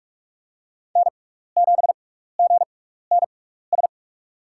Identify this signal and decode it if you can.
Morse code, heard as a keyed tone, about 46 wpm, spaced as for 15 wpm, 700 Hz. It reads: N7GNS
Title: USA Call Signs - 6x Character Spacing - 45wpm